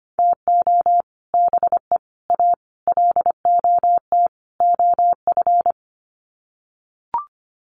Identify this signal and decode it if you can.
Morse code, heard as a keyed tone, about 25 wpm, 690 Hz. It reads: TO BE A LOT OF
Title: Sets of 5 Words Encore 25wpm